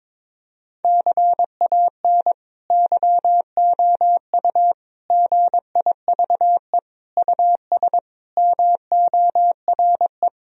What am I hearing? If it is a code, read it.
Morse code, heard as a keyed tone, about 22 wpm, 700 Hz. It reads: CAN YOU GIVE US MORE
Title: Sentences from Top 1000 Words - Review Part 3 - 22wpm